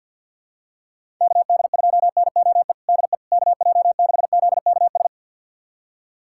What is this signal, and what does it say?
Morse code, heard as a keyed tone, about 50 wpm, 700 Hz. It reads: KD2NOE DE KJ6ZKS
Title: USA Callsign DE Callsign - 6 Characters - 50wpm